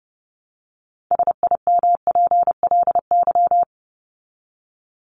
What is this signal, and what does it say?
Morse code, heard as a keyed tone, about 30 wpm, 700 Hz. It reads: SIMPLY
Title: LY Suffix - Top 100 Words with Base Words 30wpm